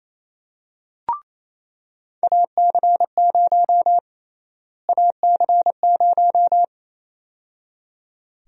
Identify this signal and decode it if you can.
Morse code, heard as a keyed tone, about 28 wpm, 700 Hz. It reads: AC0 AC0
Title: USA Call Sign Prefixes - Repeated 2x - 28wpm